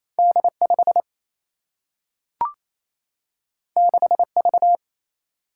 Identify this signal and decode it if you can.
Morse code, heard as a keyed tone, about 28 wpm, 700 Hz. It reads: D 5 6 V
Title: Single-Letter Single-Number 28wpm